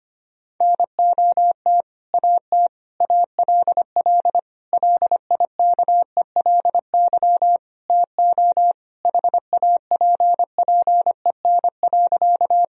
Morse code, heard as a keyed tone, about 25 wpm, 700 Hz. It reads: NOT AT ALL LIKELY TO HAPPEN.
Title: Sentences from Top 1000 Words - Review Part 4 - 25wpm